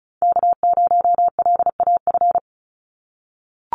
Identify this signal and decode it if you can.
Morse code, heard as a keyed tone, about 34 wpm, 700 Hz. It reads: K0LAF
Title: NA SOTA Chaser Callsigns 35wpm